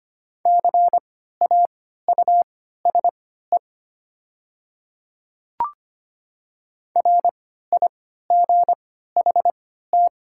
Morse code, heard as a keyed tone, about 25 wpm, spaced as for 14 wpm, 700 Hz. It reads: CAUSE RIGHT
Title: 3x Character Spacing - Top 200 Words - 25wpm